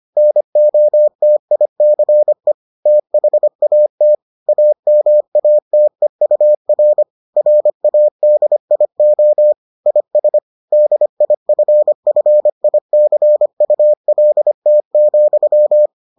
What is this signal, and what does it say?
Morse code, heard as a keyed tone, about 25 wpm, 600 Hz. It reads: NOTICE THAT AMATEUR RADIO IS DIFFICULT,